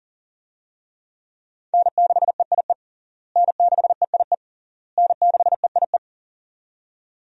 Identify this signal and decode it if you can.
Morse code, heard as a keyed tone, about 40 wpm, 700 Hz. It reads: N6EIE N6EIE N6EIE
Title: USA Call Signs - 5 Characters - Repeated 3x - 40wpm